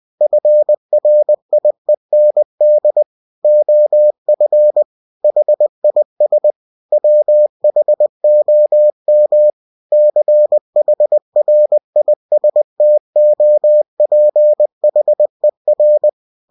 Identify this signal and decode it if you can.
Morse code, heard as a keyed tone, about 20 wpm, 600 Hz. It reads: FRIEND OF HIS WHOM CHRISTOPHER
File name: cw-books / winnie-the-pooh / winnie-20wpm-page-0106